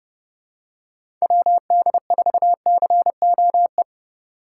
Morse code, heard as a keyed tone, about 29 wpm, 700 Hz. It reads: WD4COE